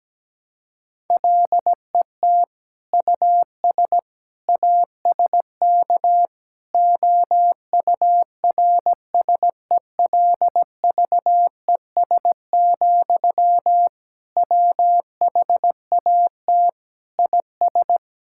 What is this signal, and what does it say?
Morse code, heard as a keyed tone, about 17 wpm, 700 Hz. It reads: LET US ASK OURSELVES, WHAT IS
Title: Sentences from Top 1600 Words 17wpm